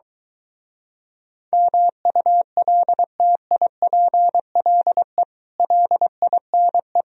Morse code, heard as a keyed tone, about 23 wpm, 700 Hz. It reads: MULTIPLE LINE